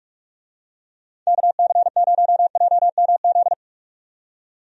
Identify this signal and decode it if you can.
Morse code, heard as a keyed tone, about 46 wpm, 700 Hz. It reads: KC0JMZ
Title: USA Call Signs - 6 Characters - 45wpm